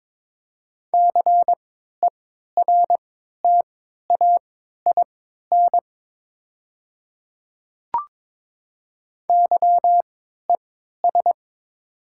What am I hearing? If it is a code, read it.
Morse code, heard as a keyed tone, about 22 wpm, spaced as for 12 wpm, 700 Hz. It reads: CERTAIN YES
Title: 3x Character Spacing - Top 500 Words - 22wpm